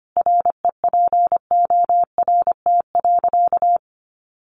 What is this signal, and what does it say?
Morse code, heard as a keyed tone, about 25 wpm, 700 Hz. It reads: REPORT.
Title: Sentences from Top 1300 Words 25wpm